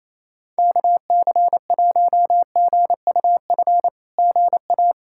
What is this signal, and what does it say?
Morse code, heard as a keyed tone, about 27 wpm, 700 Hz. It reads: KC1GUF GA
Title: SOTA QSO Practice 28wpm